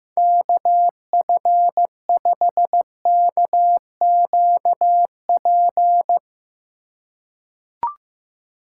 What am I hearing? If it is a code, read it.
Morse code, heard as a keyed tone, about 15 wpm, 700 Hz. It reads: KF5KQP
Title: USA Call Signs - 2x Character Spacing - 15wpm